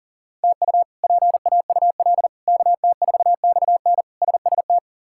Morse code, heard as a keyed tone, about 40 wpm, 700 Hz. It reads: TU PAUL KT4XN SST
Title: K1USN SST QSO Practice - 40wpm